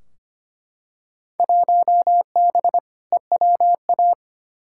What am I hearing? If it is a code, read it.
Morse code, heard as a keyed tone, about 25 wpm, 700 Hz. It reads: 1B EWA
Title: Field Day Exchanges Encore - 25wpm